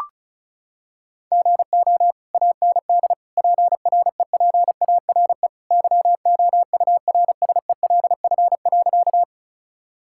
Morse code, heard as a keyed tone, about 34 wpm, 700 Hz. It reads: GO AND PREPARE YOURSELF.
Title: Sentences from Top 1000 Words - Review Part 6 - 35wpm